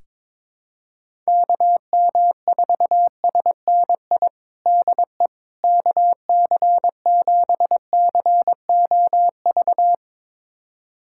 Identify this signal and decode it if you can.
Morse code, heard as a keyed tone, about 22 wpm, 700 Hz. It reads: KM4SNI DE KC7COV
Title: USA Callsign DE Callsign 22wpm